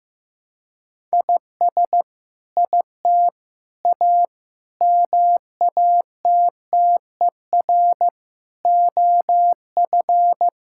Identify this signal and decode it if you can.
Morse code, heard as a keyed tone, about 15 wpm, 700 Hz. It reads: IS IT A MATTER OF
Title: Sentences from Top 1000 Words - Review Part 5 - 15wpm